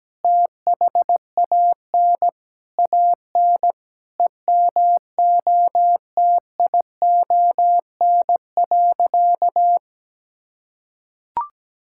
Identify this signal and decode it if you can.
Morse code, heard as a keyed tone, about 17 wpm, 700 Hz. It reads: THAN AN EMOTION.
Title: Sentences from Top 1600 Words 17wpm